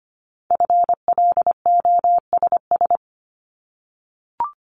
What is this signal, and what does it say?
Morse code, heard as a keyed tone, about 25 wpm, 700 Hz. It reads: FLOSS
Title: Multi-letter Phonemes 25wpm